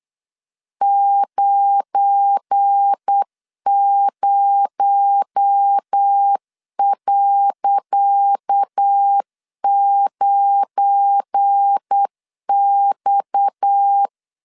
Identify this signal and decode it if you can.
Morse code, heard as a keyed tone, about 8 wpm, 800 Hz. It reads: 90.9X
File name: LEZIONI / 10 SIGNS / 10C DOT 8wpm 3649